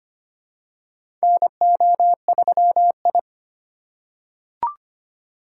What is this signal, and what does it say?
Morse code, heard as a keyed tone, about 25 wpm, 700 Hz. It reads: NO3I
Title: USA Call Signs - 4 Characters - 3x Character Spacing - 25wpm